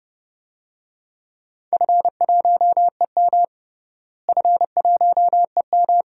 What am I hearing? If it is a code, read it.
Morse code, heard as a keyed tone, about 31 wpm, 700 Hz. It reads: F1EM F1EM
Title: DXCC-100 Call Signs - 4 Characters - Repeated 3x - 30wpm